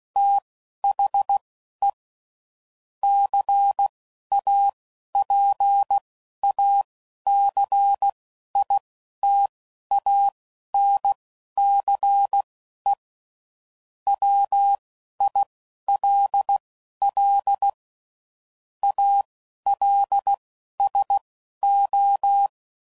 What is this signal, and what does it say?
Morse code, heard as a keyed tone, about 16 wpm, spaced as for 11 wpm, 800 Hz. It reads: THE CAPACITANCE WILL ALSO
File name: CW Code Groups @10WPM / Technical Subjects / XC